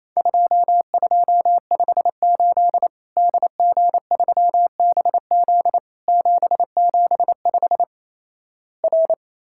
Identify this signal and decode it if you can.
Morse code, heard as a keyed tone, about 28 wpm, 700 Hz. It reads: 2258 DG3BZ 775 R
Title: Worked All Europe DX Contest - QTC Exchange - 28wpm